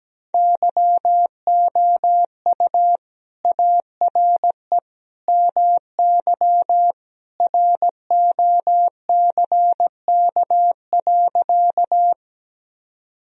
Morse code, heard as a keyed tone, about 17 wpm, 700 Hz. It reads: YOU ARE MY ROCK.
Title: Sentences from Top 1000 Words - Review Part 2 - 17wpm